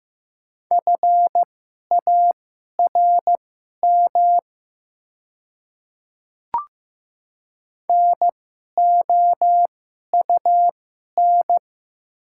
Morse code, heard as a keyed tone, about 15 wpm, spaced as for 11 wpm, 700 Hz. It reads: FARM NOUN
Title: 2x Character Spacing - Top 500 Words - 15wpm